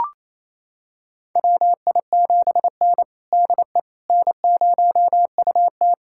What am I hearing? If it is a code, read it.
Morse code, heard as a keyed tone, about 28 wpm, 690 Hz. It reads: WI7N DE N0UT
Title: USA Callsign DE Callsign - 4 Characters - 28wpm